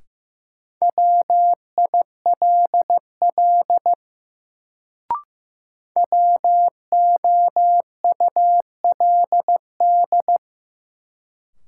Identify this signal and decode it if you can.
Morse code, heard as a keyed tone, about 15 wpm, 700 Hz. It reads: WILL WOULD
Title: Sentences from Top 100 Words - New Word Drill - 15wpm